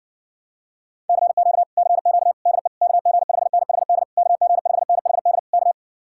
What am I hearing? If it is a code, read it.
Morse code, heard as a keyed tone, about 60 wpm, 700 Hz. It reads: CQ CQ DE KG5NHD KG5NHD K